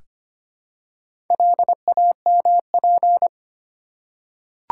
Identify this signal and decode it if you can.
Morse code, heard as a keyed tone, about 25 wpm, 700 Hz. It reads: LAMP
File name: practice / 25wpm / Easy Words with Hints - 25wpm